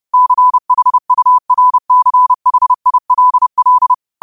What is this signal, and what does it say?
Morse code, heard as a keyed tone, about 30 wpm, 1000 Hz. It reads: CHURCHILL
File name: qotdfiles / 30wpm